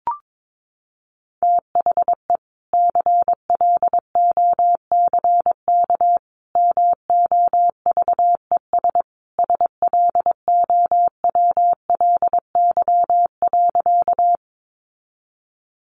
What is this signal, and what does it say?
Morse code, heard as a keyed tone, about 22 wpm, 700 Hz. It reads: THE CLOCK MOVES SLOWLY.
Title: Easy Phrases 22wpm